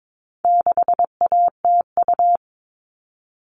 Morse code, heard as a keyed tone, about 22 wpm, 700 Hz. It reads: 6ATU
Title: All Cut Numbers - 4-Digit 22wpm